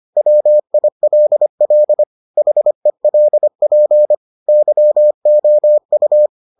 Morse code, heard as a keyed tone, about 25 wpm, 600 Hz. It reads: WILL HELP YOU